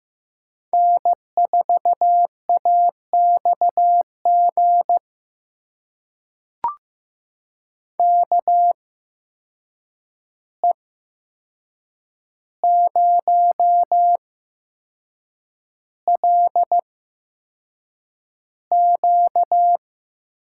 Morse code, heard as a keyed tone, about 15 wpm, spaced as for 6 wpm, 700 Hz. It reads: N4AXG KE0LQ
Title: USA Call Signs - 5 Characters - 8x Character Spacing - 15wpm